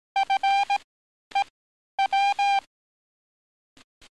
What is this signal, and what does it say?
Morse code, heard as a keyed tone, about 18 wpm, spaced as for 10 wpm, 790 Hz. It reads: FEW
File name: training / NAVPERS 11049to11060 / NAVPERS 11050 RA RB